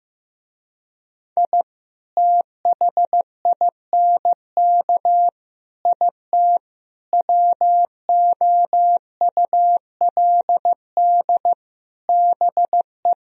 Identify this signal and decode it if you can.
Morse code, heard as a keyed tone, about 15 wpm, 700 Hz. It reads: I THINK IT WOULD BE
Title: Speed-Racing - Sets of 5 Words 15wpm